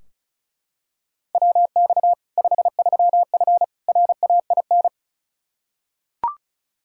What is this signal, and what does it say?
Morse code, heard as a keyed tone, about 36 wpm, 700 Hz. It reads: WX 53F RAIN
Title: Practice QSO - Exchange 2 - Info - 35wpm